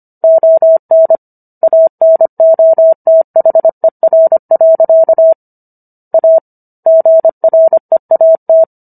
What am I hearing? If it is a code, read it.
Morse code, heard as a keyed tone, about 25 wpm, 650 Hz. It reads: ON ANOTHER. A GREAT